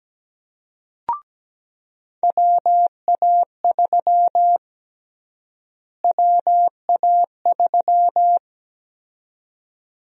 Morse code, heard as a keyed tone, about 17 wpm, 700 Hz. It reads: WA3 WA3
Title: USA Call Sign Prefixes - Repeated 2x - 17wpm